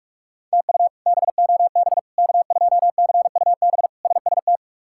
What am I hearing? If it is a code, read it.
Morse code, heard as a keyed tone, about 42 wpm, 700 Hz. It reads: TU BOB K2CUB SST
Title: K1USN SST QSO Practice - 45wpm